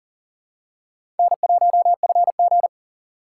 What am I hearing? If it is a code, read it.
Morse code, heard as a keyed tone, about 40 wpm, 700 Hz. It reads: N1FG
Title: USA Call Signs - 6x Character Spacing - 40wpm